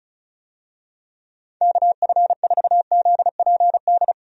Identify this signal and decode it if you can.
Morse code, heard as a keyed tone, about 36 wpm, 700 Hz. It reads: KF4ZPD
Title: USA Call Signs - 6 Characters - 35wpm